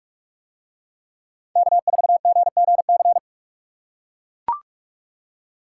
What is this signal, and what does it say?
Morse code, heard as a keyed tone, about 42 wpm, 700 Hz. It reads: K4GGC